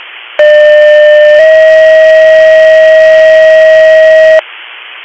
Two-tone page, heard as a one-second tone, then a three-second tone, about 600 then 640 Hz. It is heard from 0.4 to 4.4 s.